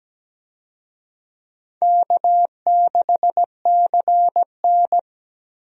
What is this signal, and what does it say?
Morse code, heard as a keyed tone, about 17 wpm, 700 Hz. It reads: K6CN
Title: USA Call Signs - 4 Characters - Repeated 3x - 17wpm